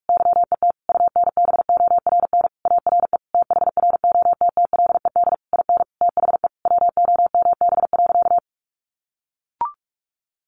Morse code, heard as a keyed tone, about 44 wpm, 700 Hz. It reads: YET UNBORN ARE THROTTLED IN THE WOMB.
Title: A Sound Of Thunder 45wpm